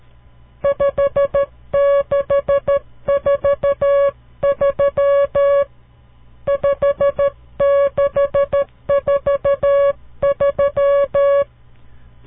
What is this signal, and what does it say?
Morse code, heard as a keyed tone, about 13 wpm, 560 Hz. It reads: 5643 5643
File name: CW Learning Programs / farnsworth / unit 5